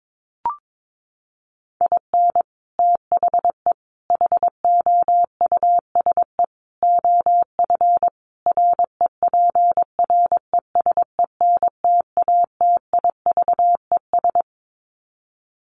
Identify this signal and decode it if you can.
Morse code, heard as a keyed tone, about 22 wpm, 700 Hz. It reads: IN THE HOUSE OF REPRESENTATIVES
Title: Sets of 5 Words Encore 22wpm